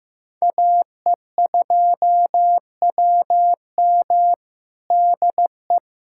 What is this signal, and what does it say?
Morse code, heard as a keyed tone, about 15 wpm, 700 Hz. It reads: AE2WM DE
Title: USA Callsign DE Callsign - 5 Characters - 15wpm